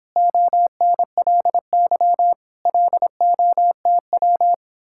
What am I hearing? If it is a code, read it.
Morse code, heard as a keyed tone, about 26 wpm, 700 Hz. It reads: ONLY LOTW